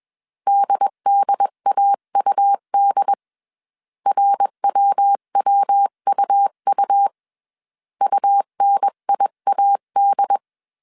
Morse code, heard as a keyed tone, about 21 wpm, 800 Hz. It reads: DDAUD RWWUU UNIAD